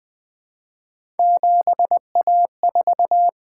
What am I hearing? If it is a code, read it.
Morse code, heard as a keyed tone, about 21 wpm, 700 Hz. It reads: 7A4
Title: Three Letter-Number Combinations - 20wpm